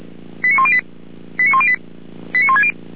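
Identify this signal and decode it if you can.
Five-tone selective call (ZVEI), heard as quick runs of five tones, tones of about 80 ms. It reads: 89108 89108 89160
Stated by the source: ZVEI five-tone selective call 89560 received on SDR.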